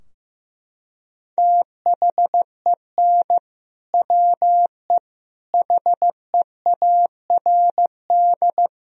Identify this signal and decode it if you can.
Morse code, heard as a keyed tone, about 15 wpm, 700 Hz. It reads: THEN WE HEARD
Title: Sentences from Top 1000 Words - Review Part 2 - 15wpm